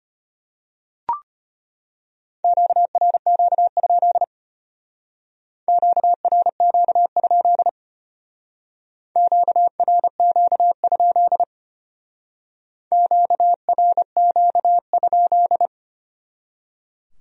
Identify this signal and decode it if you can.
Morse code, heard as a keyed tone, about 32 wpm, 700 Hz. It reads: QRQ? QRQ? QRQ? QRQ?